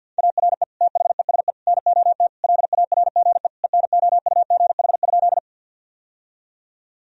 Morse code, heard as a keyed tone, about 50 wpm, 700 Hz. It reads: ARE THESE NOT LARGE ENOUGH?
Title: Sentences from Top 1000 Words - Review Part 1 - 50wpm